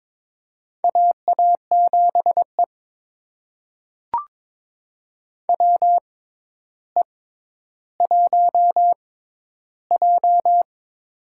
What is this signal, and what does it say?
Morse code, heard as a keyed tone, about 21 wpm, spaced as for 11 wpm, 700 Hz. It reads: AA7E WE1J
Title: USA Call Signs - 4 Characters - 6x Character Spacing - 22wpm